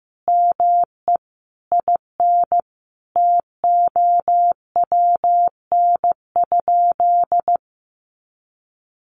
Morse code, heard as a keyed tone, about 15 wpm, 700 Hz. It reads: ME IN TOWN?
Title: Sentences from Top 500 Words 15wpm